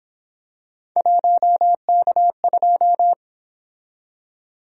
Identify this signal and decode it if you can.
Morse code, heard as a keyed tone, about 29 wpm, 700 Hz. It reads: 1K2